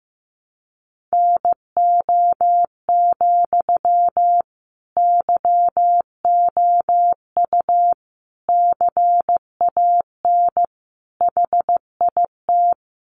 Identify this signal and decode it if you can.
Morse code, heard as a keyed tone, about 15 wpm, 700 Hz. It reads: NO, YOU CAN HIT
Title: Sentences from Top 1200 Words 15wpm